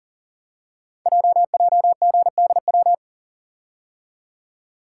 Morse code, heard as a keyed tone, about 41 wpm, 700 Hz. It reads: JJGDW
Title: Code Groups - 5 Letters - 40wpm